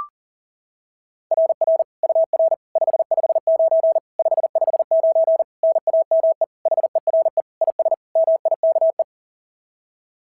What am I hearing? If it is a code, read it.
Morse code, heard as a keyed tone, about 40 wpm, 690 Hz. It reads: RR UR 559 559 NAME HERE IS MIKE
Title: SOTA QSO Practice 40wpm